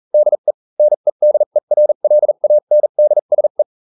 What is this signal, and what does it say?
Morse code, heard as a keyed tone, about 40 wpm, spaced as for 35 wpm, 600 Hz. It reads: DE NEDERLANDSE